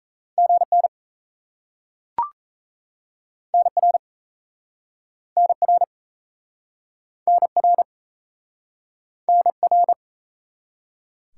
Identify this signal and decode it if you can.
Morse code, heard as a keyed tone, about 38 wpm, 700 Hz. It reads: GN NR NR NR NR